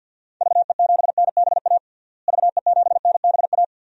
Morse code, heard as a keyed tone, about 52 wpm, 700 Hz. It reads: VE7NBA VE7NBA
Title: DXCC-100 Call Signs - 6 Characters - Repeated 2x - 50wpm